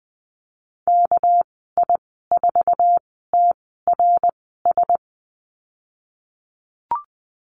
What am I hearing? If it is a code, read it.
Morse code, heard as a keyed tone, about 20 wpm, spaced as for 15 wpm, 700 Hz. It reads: KI4TRS
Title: 213 Lesson 34g - Number 8 - Call Signs 20wpm